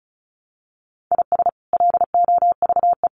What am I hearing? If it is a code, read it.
Morse code, heard as a keyed tone, about 34 wpm, 700 Hz. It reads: IS LOVE
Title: Sentences from Top 1200 Words 35wpm